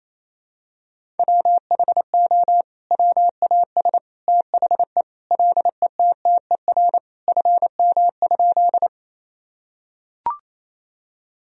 Morse code, heard as a keyed tone, about 28 wpm, 700 Hz. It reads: WHO WAS THE LETTER FM?
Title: Sentences from Top 300 Words and Abbreviations 28wpm